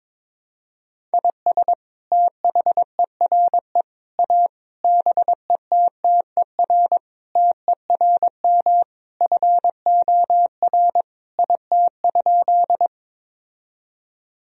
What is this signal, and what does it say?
Morse code, heard as a keyed tone, about 22 wpm, 700 Hz. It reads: IS THERE A BETTER TERM FOR IT?
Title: Sentences from Top 1000 Words - Review Part 4 - 22wpm